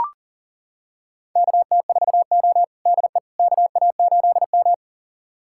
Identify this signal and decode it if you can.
Morse code, heard as a keyed tone, about 40 wpm, 690 Hz. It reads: KT4O DE KA8M
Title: USA Callsign DE Callsign - 4 Characters - 40wpm